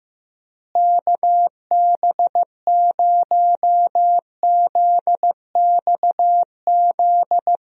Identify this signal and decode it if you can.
Morse code, heard as a keyed tone, about 15 wpm, 700 Hz. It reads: KB0ZXZ
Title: USA Call Signs - 6 Characters - Repeated 3x - 15wpm